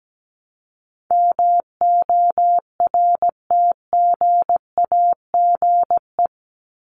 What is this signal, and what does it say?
Morse code, heard as a keyed tone, about 17 wpm, 700 Hz. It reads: MORTGAGE
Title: Top 1751-2000 Words 17wpm